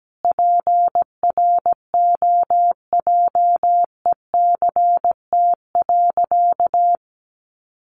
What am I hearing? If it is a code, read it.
Morse code, heard as a keyed tone, about 17 wpm, 700 Hz. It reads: PROJECT.
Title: Sentences from Top 1400 Words 17wpm